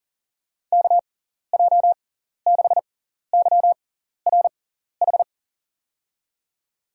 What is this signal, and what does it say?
Morse code, heard as a keyed tone, about 39 wpm, spaced as for 14 wpm, 700 Hz. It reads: KJ6YRH
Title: USA Call Signs - 6 Characters - 6x Character Spacing - 40wpm